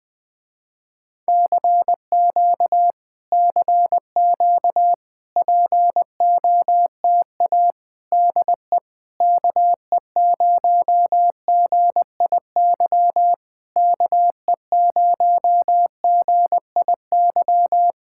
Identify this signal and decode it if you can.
Morse code, heard as a keyed tone, about 20 wpm, 700 Hz. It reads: CQ CQ POTA DE KE0GIY KE0GIY
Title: Speed-Racing Common QSO Phrases - 20wpm